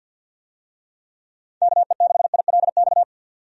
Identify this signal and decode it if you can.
Morse code, heard as a keyed tone, about 48 wpm, 700 Hz. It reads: KE6ILX
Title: USA Call Signs - 6 Characters - 50wpm